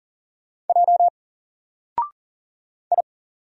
Morse code, heard as a keyed tone, about 39 wpm, 700 Hz. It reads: J I
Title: Single Letter-Number Alt 40wpm